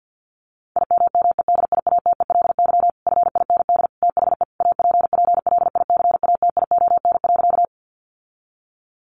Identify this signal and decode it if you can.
Morse code, heard as a keyed tone, about 50 wpm, 700 Hz. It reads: IMMEDIATELY FIND THE APPLICATION.
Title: Sentences from Top 1200 Words 50wpm